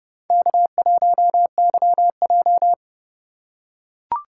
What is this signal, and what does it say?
Morse code, heard as a keyed tone, about 30 wpm, 700 Hz. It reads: K1YJ
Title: USA Call Signs - 4 Characters - 4x Character Spacing - 30wpm